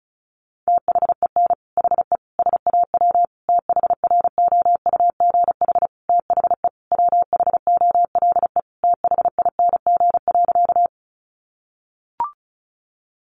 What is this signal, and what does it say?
Morse code, heard as a keyed tone, about 34 wpm, 700 Hz. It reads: THEN HE SAW THROUGH THE WHOLE THING.
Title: Sentences from Top 400 Words 35wpm